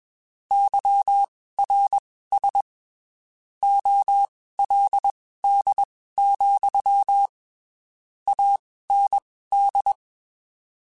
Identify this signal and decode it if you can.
Morse code, heard as a keyed tone, about 21 wpm, spaced as for 15 wpm, 800 Hz. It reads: YRS OLD, AND
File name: (spaced) / QSOs Spaced-mp3s / kc7sac @20 sp